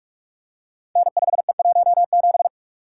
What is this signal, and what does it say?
Morse code, heard as a keyed tone, about 44 wpm, 700 Hz. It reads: N5E17
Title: Code Groups - 5 Letter:Numbers - 45wpm